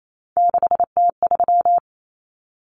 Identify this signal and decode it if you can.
Morse code, heard as a keyed tone, about 28 wpm, 700 Hz. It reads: 6T3
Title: Common Cut Numbers - 3-Digit 28wpm